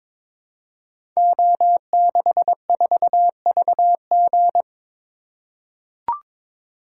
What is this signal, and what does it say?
Morse code, heard as a keyed tone, about 22 wpm, 700 Hz. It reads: O64VG
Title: Code Groups - 5 Letter:Numbers - 22wpm